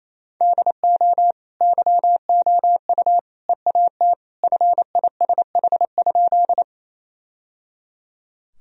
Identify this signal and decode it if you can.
Morse code, heard as a keyed tone, about 28 wpm, 700 Hz. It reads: DO YOU EAT FISH?
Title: Sentences from Top 1000 Words - Review Part 1 - 28wpm